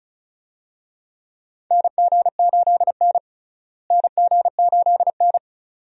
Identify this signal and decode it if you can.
Morse code, heard as a keyed tone, about 36 wpm, 700 Hz. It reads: NG8N NG8N
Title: USA Call Signs - 4 Characters - Repeated 2x - 35wpm